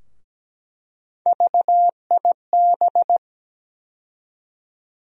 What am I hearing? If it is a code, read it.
Morse code, heard as a keyed tone, about 17 wpm, 700 Hz. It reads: VIB
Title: USA Call Sign Postfixes - 3 Letters 17wpm